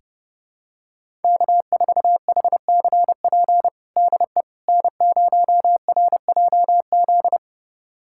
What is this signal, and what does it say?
Morse code, heard as a keyed tone, about 30 wpm, 700 Hz. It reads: K4HCP DE N0RJZ
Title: USA Callsign DE Callsign - 5 Characters - 30wpm